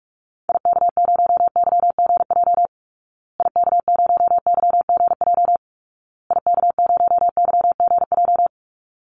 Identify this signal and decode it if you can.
Morse code, heard as a keyed tone, about 44 wpm, 700 Hz. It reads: IK0YGJ IK0YGJ IK0YGJ
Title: Call Signs Repeated 2x - 45wpm